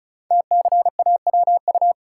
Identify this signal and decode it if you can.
Morse code, heard as a keyed tone, about 36 wpm, 700 Hz. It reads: TCAWU
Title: Code Groups - 5 Letter:Numbers - 35wpm